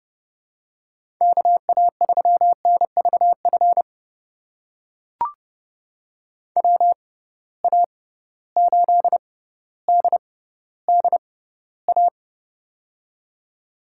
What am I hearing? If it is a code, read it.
Morse code, heard as a keyed tone, about 30 wpm, spaced as for 15 wpm, 700 Hz. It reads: KA3NVF WA8DDA
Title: USA Call Signs - 6x Character Spacing - 30wpm